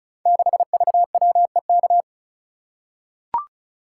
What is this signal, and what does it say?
Morse code, heard as a keyed tone, about 36 wpm, 700 Hz. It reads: 6VWEK
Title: Code Groups - 5 Letter:Numbers - 35wpm